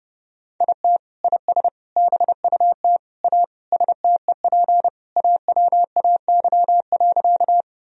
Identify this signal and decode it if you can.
Morse code, heard as a keyed tone, about 32 wpm, 700 Hz. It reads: IT IS BUT A STEP AWAY.
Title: Sentences from Top 1000 Words - Review Part 2 - 30wpm